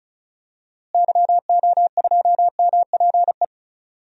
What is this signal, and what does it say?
Morse code, heard as a keyed tone, about 34 wpm, 700 Hz. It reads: YO2MPE
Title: DXCC-100 Call Signs - 6 Characters - Repeated 2x - 35wpm